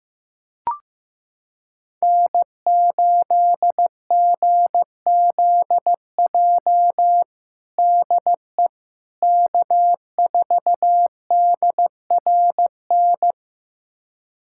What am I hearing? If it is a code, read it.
Morse code, heard as a keyed tone, about 15 wpm, 700 Hz. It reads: N8GZJ DE K4DRN
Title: USA Callsign DE Callsign 15wpm